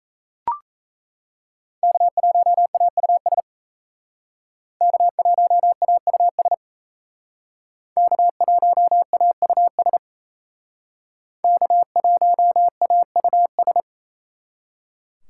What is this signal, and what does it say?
Morse code, heard as a keyed tone, about 37 wpm, 700 Hz. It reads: K1AUS K1AUS K1AUS K1AUS